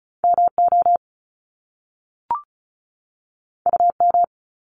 Morse code, heard as a keyed tone, about 36 wpm, 700 Hz. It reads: MO UM
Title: Top Two Letter Combinations in Words 35wpm